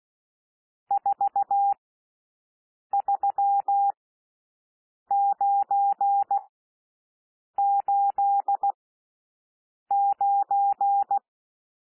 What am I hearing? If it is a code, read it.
Morse code, heard as a keyed tone, about 16 wpm, spaced as for 6 wpm, 800 Hz. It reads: 43989
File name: cwmp3 / N1 SOUND NUMBERS 16 wpm